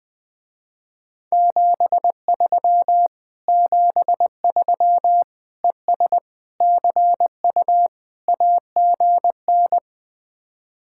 Morse code, heard as a keyed tone, about 20 wpm, 700 Hz. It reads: 73 73 ES CU AGN